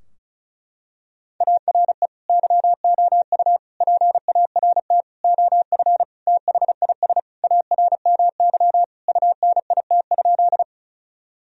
Morse code, heard as a keyed tone, about 34 wpm, 700 Hz. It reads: ARE YOU PART OF THIS ARMY UNIT?
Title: Sentences from Top 1000 Words - Review Part 2 - 35wpm